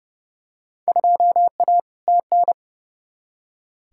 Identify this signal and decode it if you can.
Morse code, heard as a keyed tone, about 32 wpm, 700 Hz. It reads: 2A TN
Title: Field Day Exchanges - 30wpm